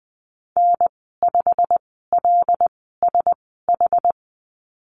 Morse code, heard as a keyed tone, about 20 wpm, spaced as for 14 wpm, 700 Hz. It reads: N5LSH
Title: 122 Lesson 20g - Number 3 - Call Signs 20wpm